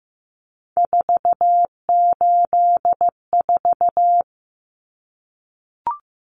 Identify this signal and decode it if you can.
Morse code, heard as a keyed tone, about 15 wpm, 700 Hz. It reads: 484
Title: Numbers - 3-Digit 15wpm